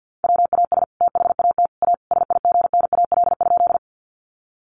Morse code, heard as a keyed tone, about 48 wpm, 700 Hz. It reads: WAS THAT A SIGNAL?
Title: Sentences from Top 1400 Words 50wpm